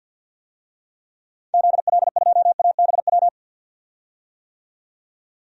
Morse code, heard as a keyed tone, about 50 wpm, 700 Hz. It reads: ZL2ABW
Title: DXCC-100 Call Signs - 6 Characters - 50wpm